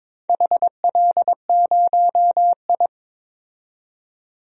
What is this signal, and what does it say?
Morse code, heard as a keyed tone, about 22 wpm, 690 Hz. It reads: HL0I
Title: DXCC-100 Call Signs - 4 Characters - 22wpm